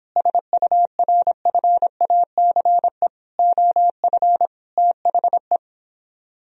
Morse code, heard as a keyed tone, about 29 wpm, 700 Hz. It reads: SURFACE OF THE